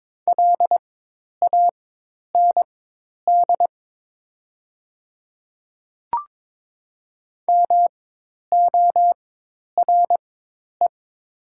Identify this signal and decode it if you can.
Morse code, heard as a keyed tone, about 22 wpm, spaced as for 10 wpm, 700 Hz. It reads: LAND MORE
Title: 4x Character Spacing - Top 200 Words - 22wpm